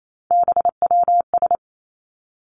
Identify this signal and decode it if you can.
Morse code, heard as a keyed tone, about 28 wpm, 700 Hz. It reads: BWS
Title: Three Challenging Letter-Number Combinations 28wpm